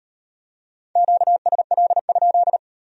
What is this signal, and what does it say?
Morse code, heard as a keyed tone, about 38 wpm, 700 Hz. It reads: QSL?